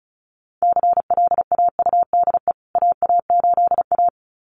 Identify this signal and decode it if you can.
Morse code, heard as a keyed tone, about 34 wpm, 700 Hz. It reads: CLAUDE AA8A
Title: CWT Exchange 35wpm